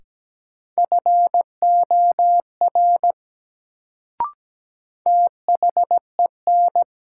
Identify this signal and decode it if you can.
Morse code, heard as a keyed tone, about 17 wpm, 700 Hz. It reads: FOR THEN
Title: Sentences from Top 100 Words - New Word Drill - 17wpm